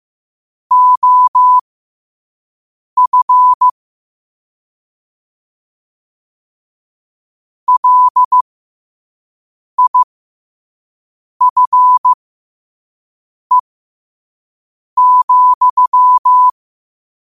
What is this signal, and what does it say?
Morse code, heard as a keyed tone, about 15 wpm, spaced as for 5 wpm, 1000 Hz. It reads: OF LIFE,